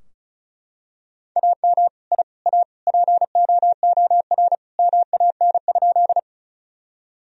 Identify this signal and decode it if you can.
Morse code, heard as a keyed tone, about 36 wpm, 700 Hz. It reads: AM I A POOR MAN?
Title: Sentences from Top 1000 Words - Review Part 2 - 35wpm